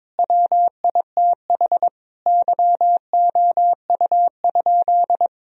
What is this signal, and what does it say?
Morse code, heard as a keyed tone, about 22 wpm, 700 Hz. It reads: WITH YOU?
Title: Sentences from Top 1000 Words - Review Part 3 - 22wpm